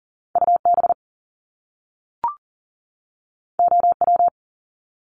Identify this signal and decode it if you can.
Morse code, heard as a keyed tone, about 39 wpm, 700 Hz. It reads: UB OW
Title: Top Two Letter Combinations in Words 40wpm